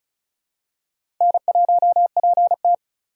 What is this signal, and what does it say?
Morse code, heard as a keyed tone, about 34 wpm, 700 Hz. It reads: N1PT
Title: USA Call Signs - 4 Characters - 4x Character Spacing - 35wpm